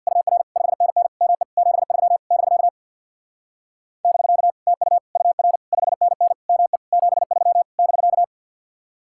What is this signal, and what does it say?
Morse code, heard as a keyed tone, about 60 wpm, 700 Hz. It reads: UR 5NN ME 73 <BK> <BK> TU UR 5NN ME 73 <BK>
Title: Speed-Racing Common QSO Phrases - 50wpm